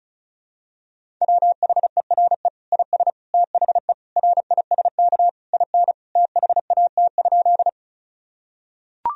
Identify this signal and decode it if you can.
Morse code, heard as a keyed tone, about 36 wpm, 700 Hz. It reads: WHERE IS THE RISK IN THAT?
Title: Sentences from Top 1000 Words - Review Part 4 - 35wpm